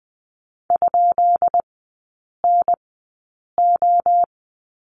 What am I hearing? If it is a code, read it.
Morse code, heard as a keyed tone, about 20 wpm, spaced as for 8 wpm, 690 Hz. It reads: ?NO
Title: 133 Lesson 22e - Question Mark - Three Characters 20wpm